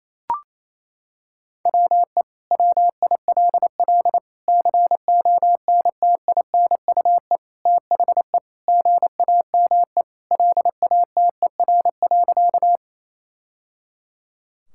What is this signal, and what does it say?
Morse code, heard as a keyed tone, about 28 wpm, 700 Hz. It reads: WE WILL CONTINUE THE GAME LATER.
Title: Sentences from Top 1000 Words - Review Part 6 - 28wpm